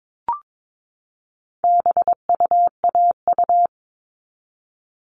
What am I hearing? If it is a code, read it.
Morse code, heard as a keyed tone, about 22 wpm, 700 Hz. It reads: BUAU
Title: All Cut Numbers - 4-Digit 22wpm